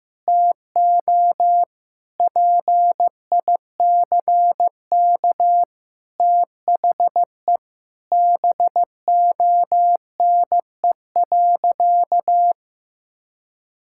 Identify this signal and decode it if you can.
Morse code, heard as a keyed tone, about 15 wpm, 700 Hz. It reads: TO PICK THE BONE.
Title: Sentences from Top 1600 Words 15wpm